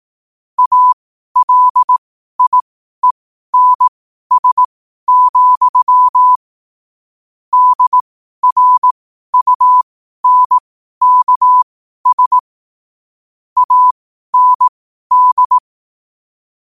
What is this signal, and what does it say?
Morse code, heard as a keyed tone, about 18 wpm, spaced as for 12 wpm, 1000 Hz. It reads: ALIENS, DRUNKS AND
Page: New quotes every day in morse code at 12 Words per minute.